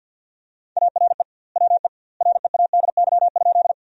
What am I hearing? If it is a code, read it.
Morse code, heard as a keyed tone, about 50 wpm, 700 Hz. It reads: ARE WE READY?
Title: 2x Word-Spacing - Sentences from Top 400 Words - 50wpm